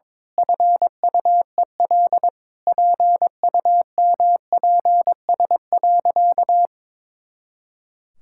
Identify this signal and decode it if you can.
Morse code, heard as a keyed tone, about 22 wpm, 700 Hz. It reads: FUEL PUMPS.
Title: Pangram Challenge v2 - 22wpm